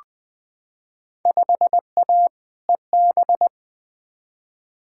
Morse code, heard as a keyed tone, about 20 wpm, 700 Hz. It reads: 5A EB
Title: Field Day Exchanges Encore - 20wpm